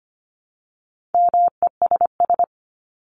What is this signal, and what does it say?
Morse code, heard as a keyed tone, about 25 wpm, 700 Hz. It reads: MESS
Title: Top 1751-2000 Words 25wpm